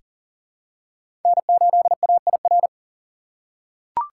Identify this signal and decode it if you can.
Morse code, heard as a keyed tone, about 38 wpm, 700 Hz. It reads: N8AIR
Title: USA Call Signs - 5 Characters - 6x Character Spacing - 40wpm